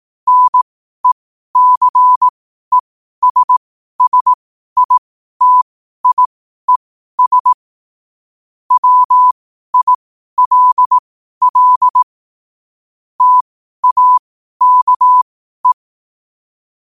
Morse code, heard as a keyed tone, about 18 wpm, spaced as for 12 wpm, 1000 Hz. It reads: NECESSITIES WILL TAKE
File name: qotdfiles / 12wpm